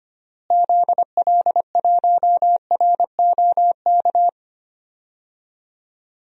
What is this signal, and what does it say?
Morse code, heard as a keyed tone, about 25 wpm, 700 Hz. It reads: ZL1ROK
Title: DXCC-100 Call Signs - 6 Characters - Repeated 3x - 25wpm